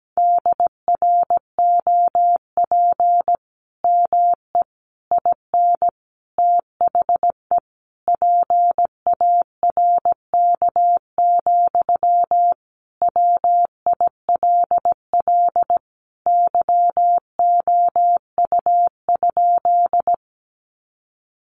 Morse code, heard as a keyed tone, about 17 wpm, 700 Hz. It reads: DROP ME IN THE PARK, WILL YOU?
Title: Sentences from Top 1400 Words 17wpm